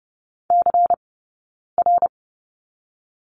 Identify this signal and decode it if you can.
Morse code, heard as a keyed tone, about 30 wpm, 700 Hz. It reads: C R
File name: course / 30:10wpm / 099 Lesson 17d - Letter u - Two Characters 30wpm